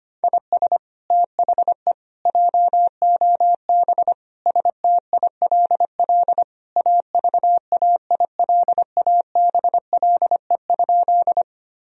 Morse code, heard as a keyed tone, about 25 wpm, 700 Hz. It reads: IS THE JOB STILL AVAILABLE?
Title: Sentences from Top 1000 Words - Review Part 3 - 25wpm